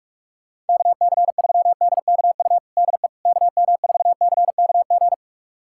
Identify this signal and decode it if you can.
Morse code, heard as a keyed tone, about 46 wpm, 700 Hz. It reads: KC3DKU DE KM4CKG
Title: USA Callsign DE Callsign 45wpm